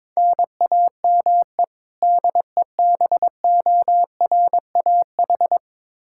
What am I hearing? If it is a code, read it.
Morse code, heard as a keyed tone, about 22 wpm, 700 Hz. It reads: NAME DEBORAH
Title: Practice QSO - Exchange 1 - Beginning - 22wpm